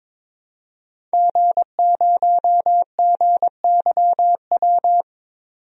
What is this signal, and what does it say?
Morse code, heard as a keyed tone, about 22 wpm, 700 Hz. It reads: G0GYW
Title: DXCC-100 Call Signs - 5 Characters - 22wpm